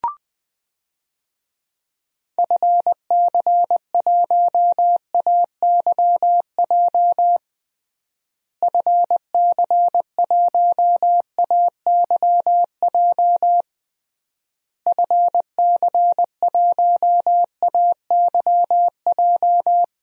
Morse code, heard as a keyed tone, about 21 wpm, 700 Hz. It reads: FC1AYJ FC1AYJ FC1AYJ
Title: DXCC-100 Call Signs - 6 Characters - Repeated 3x - 20wpm